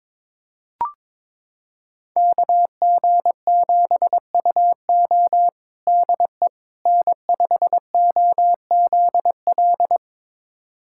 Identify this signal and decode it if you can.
Morse code, heard as a keyed tone, about 22 wpm, 700 Hz. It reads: KG7UO DE N5OZL
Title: USA Callsign DE Callsign - 5 Characters - 22wpm